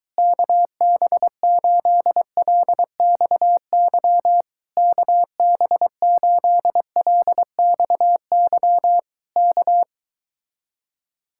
Morse code, heard as a keyed tone, about 23 wpm, 700 Hz. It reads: KB8LXY KB8LXY K